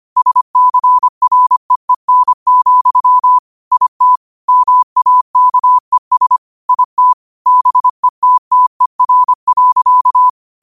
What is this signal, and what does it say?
Morse code, heard as a keyed tone, about 25 wpm, 1000 Hz. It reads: SCREEN, IT MAKES IT BETTER.
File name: qotdfiles / 25wpm